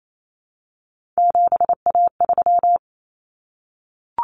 Morse code, heard as a keyed tone, about 28 wpm, 700 Hz. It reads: 7A3
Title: Common Cut Numbers - 3-Digit 28wpm